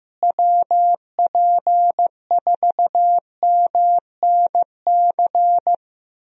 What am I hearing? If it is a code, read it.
Morse code, heard as a keyed tone, about 15 wpm, 700 Hz. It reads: WP4MNC
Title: USA Call Signs - 6 Characters - Repeated 2x - 15wpm